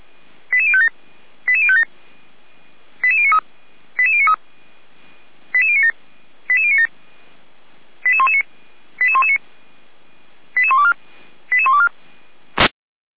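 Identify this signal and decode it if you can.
Five-tone selective call (ZVEI), heard as quick runs of five tones, tones of about 70 ms. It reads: 88057 88057 88093 88093 88097 88097 88119 88119 88124 88124
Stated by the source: ZVEI five-tone fire brigade alert received on an SDR, from Feuerwehr Orlamünde.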